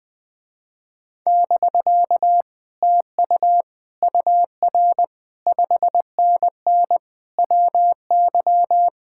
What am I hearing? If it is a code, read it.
Morse code, heard as a keyed tone, about 20 wpm, 700 Hz. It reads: <BK> TU UR 5NN WY
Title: Speed-Racing Common QSO Phrases - 20wpm